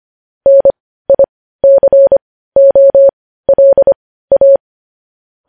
Morse code, heard as a keyed tone, about 25 wpm, spaced as for 15 wpm, 550 Hz. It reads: NICOLA
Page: news headlines in morse code at 15 wpm